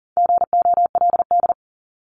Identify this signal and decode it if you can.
Morse code, heard as a keyed tone, about 40 wpm, 700 Hz. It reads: GOLD
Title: Top 500 Words 40wpm